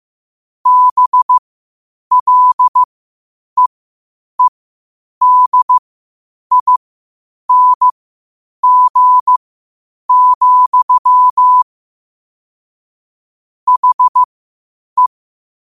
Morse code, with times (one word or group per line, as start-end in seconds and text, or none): 0.7-11.6 BLEEDING,
13.7-15.1 HE